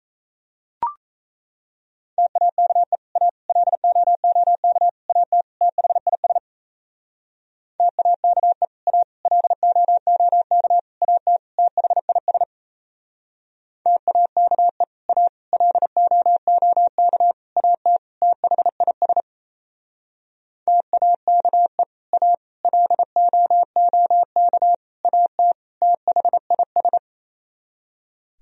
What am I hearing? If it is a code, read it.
Morse code, heard as a keyed tone, about 38 wpm, 700 Hz. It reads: TAKE A LOOK AT THIS TAKE A LOOK AT THIS TAKE A LOOK AT THIS TAKE A LOOK AT THIS